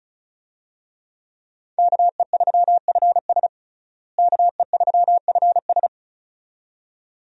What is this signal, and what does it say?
Morse code, heard as a keyed tone, about 34 wpm, 700 Hz. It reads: KE3FS KE3FS
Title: USA Call Signs - 5 Characters - Repeated 2x - 35wpm